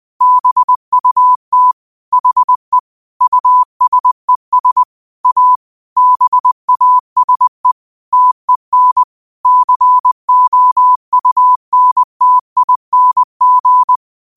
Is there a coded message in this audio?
Yes: Morse code from 0.2 to 14.0 s, heard as a keyed tone, about 20 wpm, 1000 Hz. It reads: BUT HE USES A BASE TEN COUNTING